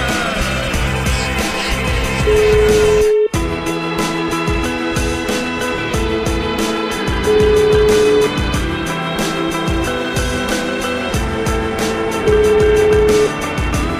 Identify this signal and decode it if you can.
Ringback tone (Europe, 425 Hz), with noise.